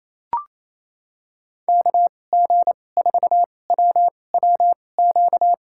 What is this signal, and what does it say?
Morse code, heard as a keyed tone, about 28 wpm, spaced as for 20 wpm, 700 Hz. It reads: KG4WWQ
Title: USA Call Signs - 6 Characters - 2x Character Spacing - 28wpm